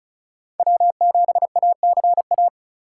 Morse code, heard as a keyed tone, about 36 wpm, 700 Hz. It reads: W7ACA
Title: USA Call Signs - 2x Character Spacing - 35wpm